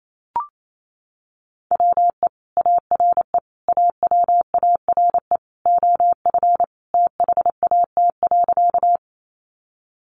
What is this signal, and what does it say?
Morse code, heard as a keyed tone, about 28 wpm, 700 Hz. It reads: WE ARE AWARE OF THAT.
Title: Sentences from Top 1200 Words 28wpm